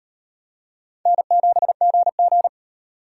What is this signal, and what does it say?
Morse code, heard as a keyed tone, about 35 wpm, 700 Hz. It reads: N7GG